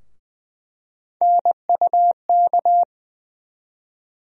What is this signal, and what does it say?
Morse code, heard as a keyed tone, about 20 wpm, 700 Hz. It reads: NUK
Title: USA Call Sign Postfixes - 3 Letters 20wpm